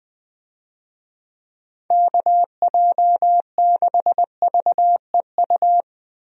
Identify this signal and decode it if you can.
Morse code, heard as a keyed tone, about 20 wpm, 700 Hz. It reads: KJ6VEU
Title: USA Call Signs - 6 Characters - Repeated 3x - 20wpm